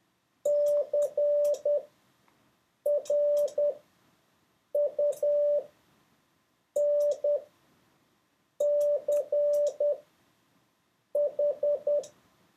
Morse code, heard as a keyed tone, about 10 wpm, spaced as for 5 wpm, 580 Hz. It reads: CRUNCH
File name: 4of6-words-1